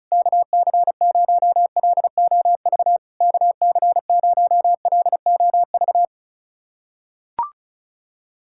Morse code, heard as a keyed tone, about 34 wpm, 690 Hz. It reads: KC0LOV KC0LOV
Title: SOTA QSO Practice 35wpm